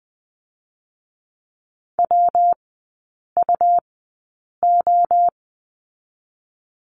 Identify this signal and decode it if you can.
Morse code, heard as a keyed tone, about 20 wpm, 700 Hz. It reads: W U O
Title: 114 Lesson 19e - Letter w - Three Characters 20wpm